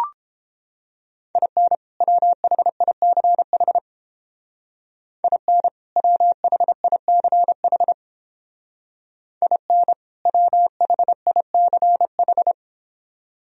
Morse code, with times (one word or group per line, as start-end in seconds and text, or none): 1.4-1.8 IN
2.0-3.8 WHICH
5.2-5.7 IN
6.0-7.9 WHICH
9.4-9.9 IN
10.3-12.5 WHICH